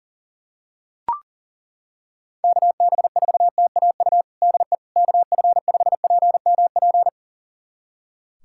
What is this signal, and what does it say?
Morse code, heard as a keyed tone, about 37 wpm, 700 Hz. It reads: KB4TAU DE KF5PMP
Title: USA Callsign DE Callsign - 6 Characters - 40wpm